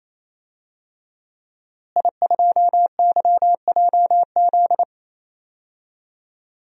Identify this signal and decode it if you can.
Morse code, heard as a keyed tone, about 28 wpm, 700 Hz. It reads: I2YJZ
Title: DXCC-100 Call Signs - 5 Characters - Repeated 3x - 28wpm